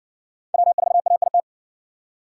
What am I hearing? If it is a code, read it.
Morse code, heard as a keyed tone, about 52 wpm, 700 Hz. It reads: W4AIT